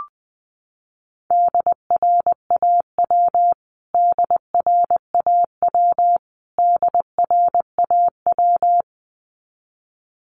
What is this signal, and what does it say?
Morse code, heard as a keyed tone, about 20 wpm, 700 Hz. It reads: DRAW DRAW DRAW
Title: Top 500 Words Repeated 20wpm